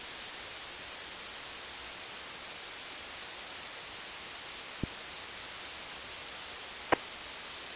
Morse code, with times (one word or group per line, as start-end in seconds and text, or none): none